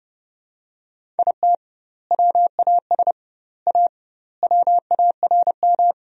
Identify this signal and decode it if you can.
Morse code, heard as a keyed tone, about 30 wpm, 700 Hz. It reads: IT WAS A WARM
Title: 2x Word-Spacing - Sentences from Top 500 Words - 30wpm